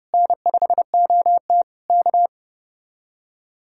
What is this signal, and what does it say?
Morse code, heard as a keyed tone, about 33 wpm, 700 Hz. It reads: N5OT K